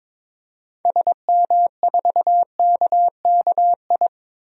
Morse code, heard as a keyed tone, about 22 wpm, 700 Hz. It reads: SM4KKI
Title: DXCC-100 Call Signs - 6 Characters - 22wpm